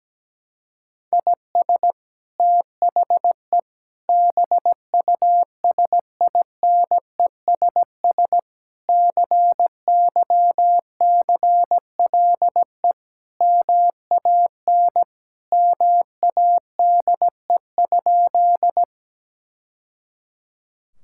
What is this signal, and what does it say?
Morse code, heard as a keyed tone, about 17 wpm, 700 Hz. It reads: IS THE BUSINESS CYCLE MAN MADE?
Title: Sentences from Top 1000 Words - Review Part 6 - 17wpm